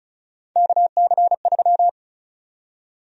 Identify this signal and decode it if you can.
Morse code, heard as a keyed tone, about 36 wpm, 700 Hz. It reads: KC3
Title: USA Call Sign Prefixes - Repeated 2x - 35wpm